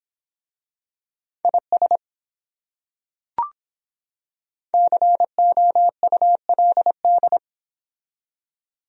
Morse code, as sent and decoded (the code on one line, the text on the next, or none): .. ... / -.-. --- ..- .-.. -..
IS COULD